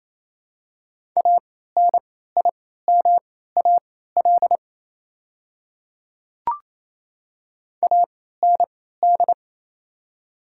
Morse code, heard as a keyed tone, about 28 wpm, spaced as for 16 wpm, 700 Hz. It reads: ANIMAL AND
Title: 3x Character Spacing - Top 200 Words - 28wpm